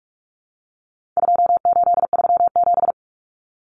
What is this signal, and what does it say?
Morse code, heard as a keyed tone, about 44 wpm, 700 Hz. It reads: 2837
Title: Common Cut Numbers - 4-Digit 45wpm